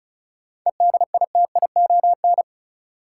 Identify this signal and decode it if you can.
Morse code, heard as a keyed tone, about 36 wpm, 700 Hz. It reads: EDITION
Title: Sets of 5 Words - New Word Drill - 35wpm